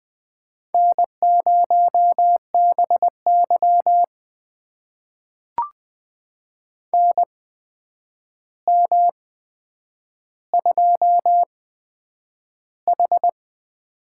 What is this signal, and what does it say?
Morse code, heard as a keyed tone, about 20 wpm, spaced as for 9 wpm, 700 Hz. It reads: N0BY NM2H
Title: USA Call Signs - 4 Characters - 8x Character Spacing - 20wpm